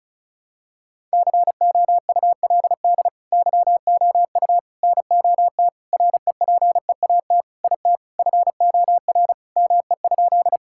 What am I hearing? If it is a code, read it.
Morse code, heard as a keyed tone, about 36 wpm, 700 Hz. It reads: COULD YOU NOT REPEAT IT FOR ME?
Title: Sentences from Top 1600 Words 35wpm